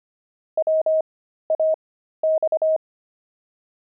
Morse code, heard as a keyed tone, about 25 wpm, spaced as for 13 wpm, 630 Hz. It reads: WAX
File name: pangram-fs-13-cs-25